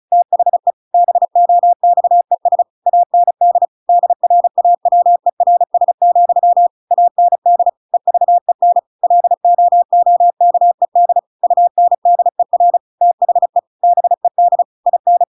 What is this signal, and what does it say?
Morse code, heard as a keyed tone, about 34 wpm, 700 Hz. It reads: THE BOXES AND DRAWERS, AND EVEN LOOKED UNDER THE BED IN